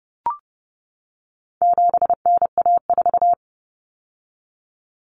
Morse code, heard as a keyed tone, about 30 wpm, 700 Hz. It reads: 7NA4
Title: Common Cut Numbers - 4-Digit 30wpm